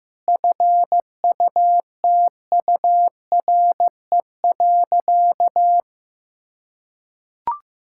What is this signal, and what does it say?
Morse code, heard as a keyed tone, about 15 wpm, 700 Hz. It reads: FUTURE.
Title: Sentences from Top 1000 Words - Review Part 5 - 15wpm